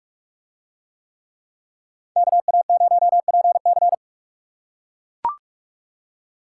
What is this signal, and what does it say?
Morse code, heard as a keyed tone, about 40 wpm, 700 Hz. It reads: KA0PC